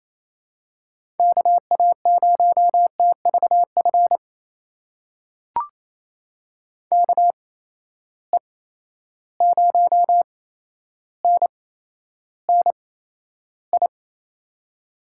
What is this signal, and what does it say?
Morse code, heard as a keyed tone, about 28 wpm, spaced as for 12 wpm, 700 Hz. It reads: KA0TVF KE0NNI
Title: USA Call Signs - 6 Characters - 8x Character Spacing - 28wpm